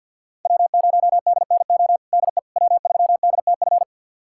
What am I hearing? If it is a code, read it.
Morse code, heard as a keyed tone, about 52 wpm, 700 Hz. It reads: W0DNO DE W3DTF
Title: USA Callsign DE Callsign - 5 Characters - 50wpm